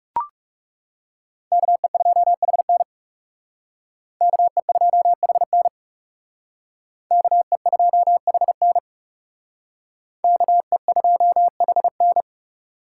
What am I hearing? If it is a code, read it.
Morse code, heard as a keyed tone, about 37 wpm, 700 Hz. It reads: KE2HN KE2HN KE2HN KE2HN
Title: Speed Racing - USA Call Signs 30wpm